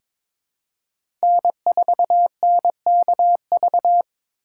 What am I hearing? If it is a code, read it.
Morse code, heard as a keyed tone, about 22 wpm, 700 Hz. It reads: N4NKV
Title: USA Call Signs - 4x Character Spacing - 22wpm